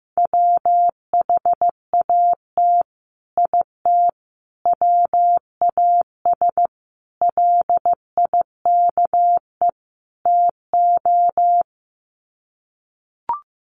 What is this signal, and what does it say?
Morse code, heard as a keyed tone, about 15 wpm, 700 Hz. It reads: WHAT IT WAS LIKE TO
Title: Sets of 5 Words Encore 15wpm